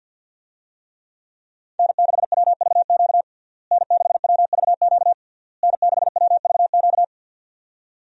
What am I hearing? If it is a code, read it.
Morse code, heard as a keyed tone, about 47 wpm, 700 Hz. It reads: N6WVQ N6WVQ N6WVQ
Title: USA Call Signs - 5 Characters - Repeated 3x - 50wpm